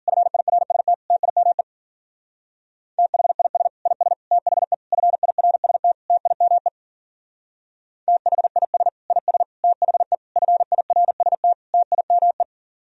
Morse code, heard as a keyed tone, about 50 wpm, 700 Hz. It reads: FIRST TIME THIS IS THE FIRST TIME THIS IS THE FIRST TIME
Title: Speed-Racing - Sets of 5 Words 40wpm